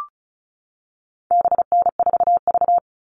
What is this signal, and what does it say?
Morse code, heard as a keyed tone, about 34 wpm, 700 Hz. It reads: BN4V
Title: All Cut Numbers - 4-Digit 35wpm